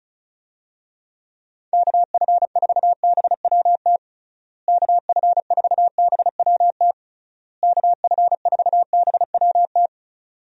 Morse code, heard as a keyed tone, about 36 wpm, 700 Hz. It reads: KF4BWT KF4BWT KF4BWT
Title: USA Call Signs - 6 Characters - Repeated 3x - 35wpm